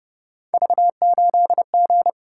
Morse code, heard as a keyed tone, about 29 wpm, 700 Hz. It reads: V8G